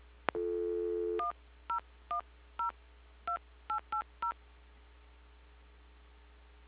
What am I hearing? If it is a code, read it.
DTMF, heard as telephone touch tones, tones of about 110 ms. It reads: 10102880
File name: recordings / PortlandUnion Millennium